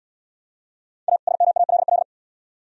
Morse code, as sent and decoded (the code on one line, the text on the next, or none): .- / ... -- .- .-.. .-..
A SMALL